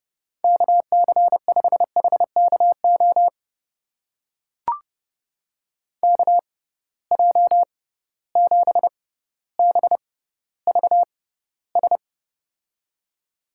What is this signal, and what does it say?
Morse code, heard as a keyed tone, about 30 wpm, spaced as for 15 wpm, 700 Hz. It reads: KC5HKO KJ7BVS
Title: USA Call Signs - 6 Characters - 6x Character Spacing - 30wpm